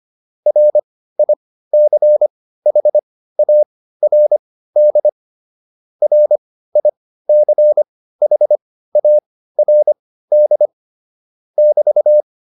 Morse code, heard as a keyed tone, about 25 wpm, spaced as for 15 wpm, 600 Hz. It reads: RICHARD RICHARD <BT>